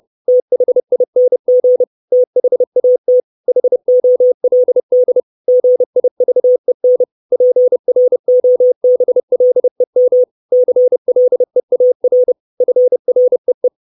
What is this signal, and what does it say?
Morse code, heard as a keyed tone, about 30 wpm, 500 Hz. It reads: THING THAT HOLD GIVEN PROBLEM CLEAR FREE
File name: 632 Words 30WPM